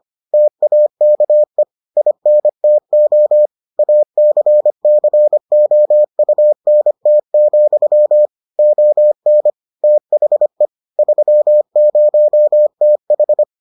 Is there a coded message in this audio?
Yes: Morse code from 0.3 to 13.4 s, heard as a keyed tone, about 25 wpm, 600 Hz. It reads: TAKE INTO ACCOUNT, ON THE 30TH